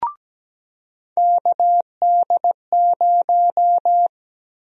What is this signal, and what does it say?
Morse code, heard as a keyed tone, about 17 wpm, 700 Hz. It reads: KD0
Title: USA Call Sign Prefixes - Repeated 2x - 17wpm